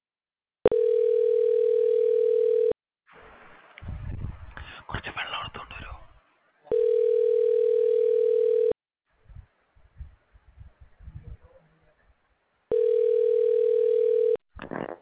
Ringback tone (North America), 440 + 480 Hz.